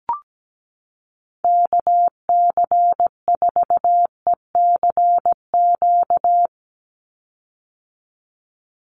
Morse code, heard as a keyed tone, about 17 wpm, 700 Hz. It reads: KC4ECQ
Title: USA Call Signs 17wpm